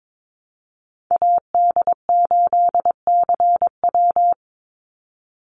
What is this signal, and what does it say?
Morse code, heard as a keyed tone, about 22 wpm, 700 Hz. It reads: AD8CW
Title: Call Signs Repeated 3x - 22wpm